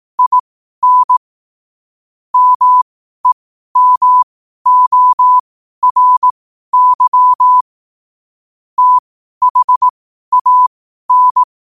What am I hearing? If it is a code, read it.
Morse code, heard as a keyed tone, about 18 wpm, spaced as for 12 wpm, 1000 Hz. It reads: IN MEMORY THAN
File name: qotdfiles / 12wpm